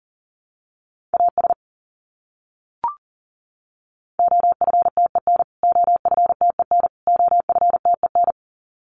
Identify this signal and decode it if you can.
Morse code, heard as a keyed tone, about 40 wpm, 700 Hz. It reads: AS OFTEN OFTEN OFTEN
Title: Top 500 Words Repeated 40wpm